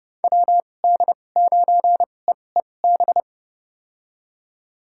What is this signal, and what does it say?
Morse code, heard as a keyed tone, about 32 wpm, spaced as for 22 wpm, 700 Hz. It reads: WD9EEB
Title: USA Call Signs - 6 Characters - 2x Character Spacing - 30wpm